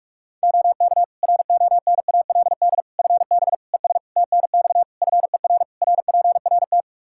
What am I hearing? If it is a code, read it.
Morse code, heard as a keyed tone, about 47 wpm, 700 Hz. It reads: OK RONALD FB ES TNX FER RPRT
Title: Practice QSO - Exchange 2 - Info - 45wpm